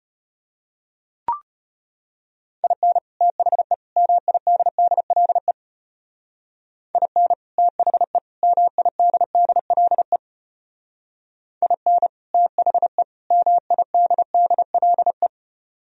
Morse code, heard as a keyed tone, about 32 wpm, 700 Hz. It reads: IN THE MIDDLE IN THE MIDDLE IN THE MIDDLE